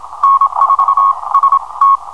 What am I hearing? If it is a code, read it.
Morse code, heard as a keyed tone, about 27 wpm, spaced as for 22 wpm, 1100 Hz. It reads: N4ST
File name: n4st / hum cw